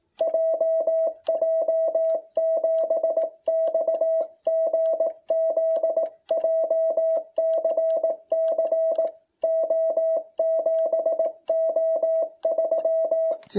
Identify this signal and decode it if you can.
Morse code, heard as a keyed tone, about 18 wpm, 660 Hz. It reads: JJ7XGZ JCC O7O3